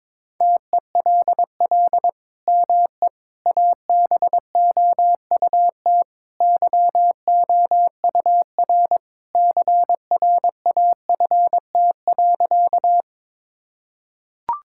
Morse code, heard as a keyed tone, about 21 wpm, 700 Hz. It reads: TELL ME ABOUT YOUR CRAFT.
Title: Sentences from Top 1000 Words - Review Part 6 - 22wpm